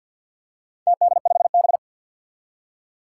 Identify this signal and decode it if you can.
Morse code, heard as a keyed tone, about 53 wpm, 700 Hz. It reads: TD5B
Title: Four Challenging Letter-Number Combinations 50wpm